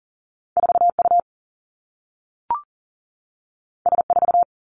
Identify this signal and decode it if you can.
Morse code, heard as a keyed tone, about 40 wpm, 700 Hz. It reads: 4U S4
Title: Two Challenging Letter-Number Combinations 40wpm